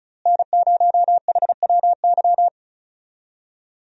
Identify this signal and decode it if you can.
Morse code, heard as a keyed tone, about 34 wpm, 700 Hz. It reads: N0HWY
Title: USA Call Signs - 5 Characters - Repeated 3x - 35wpm